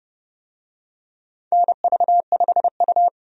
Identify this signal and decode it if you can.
Morse code, heard as a keyed tone, about 32 wpm, 700 Hz. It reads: NV5U
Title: USA Call Signs - 4 Characters - Repeated 3x - 30wpm